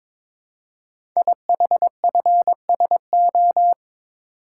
Morse code, heard as a keyed tone, about 22 wpm, 700 Hz. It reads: IHFSO
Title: Code Groups - 5 Letters - 22wpm